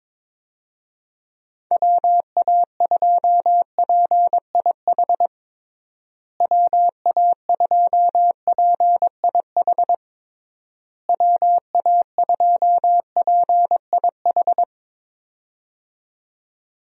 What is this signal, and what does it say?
Morse code, heard as a keyed tone, about 22 wpm, 700 Hz. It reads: WA2PIH WA2PIH WA2PIH
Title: USA Call Signs - 6 Characters - Repeated 3x - 22wpm